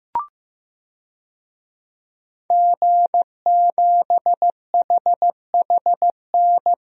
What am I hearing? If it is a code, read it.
Morse code, heard as a keyed tone, about 15 wpm, 700 Hz. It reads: G7HHN
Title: DXCC-100 Call Signs - 5 Characters - Repeated 2x - 15wpm